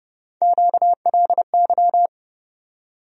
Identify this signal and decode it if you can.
Morse code, heard as a keyed tone, about 30 wpm, 700 Hz. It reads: QLY
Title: USA Call Sign Postfixes - 3 Letters 30wpm